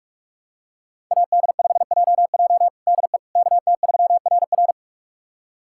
Morse code, heard as a keyed tone, about 41 wpm, 700 Hz. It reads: AD5JJ DE KT3RR
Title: USA Callsign DE Callsign - 5 Characters - 45wpm